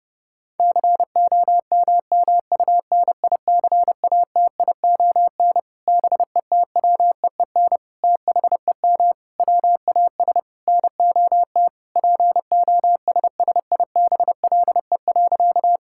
Morse code, heard as a keyed tone, about 32 wpm, 700 Hz. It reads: COMMUNICATION BETWEEN THEM WAS NOT POSSIBLE.
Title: Sentences from Top 1000 Words - Review Part 6 - 30wpm